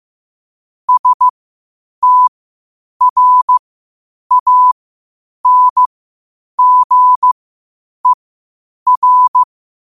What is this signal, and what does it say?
Morse code, heard as a keyed tone, about 15 wpm, spaced as for 8 wpm, 1000 Hz. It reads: STRANGER
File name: qotdfiles / 8wpm